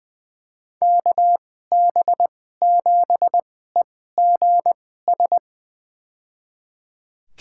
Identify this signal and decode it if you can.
Morse code, heard as a keyed tone, about 20 wpm, spaced as for 14 wpm, 700 Hz. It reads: KB7EGS
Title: USA Call Signs - 6 Characters - 2x Character Spacing - 20wpm